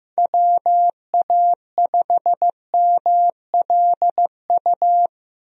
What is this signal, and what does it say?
Morse code, heard as a keyed tone, about 14 wpm, 700 Hz. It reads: WA5MLU
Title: USA Call Signs - 6 Characters - Repeated 3x - 15wpm